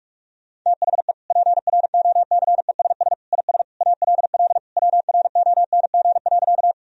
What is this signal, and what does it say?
Morse code, heard as a keyed tone, about 46 wpm, 700 Hz. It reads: THE PROCESS IS ALL WRONG.
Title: Sentences from Top 1000 Words - Review Part 3 - 45wpm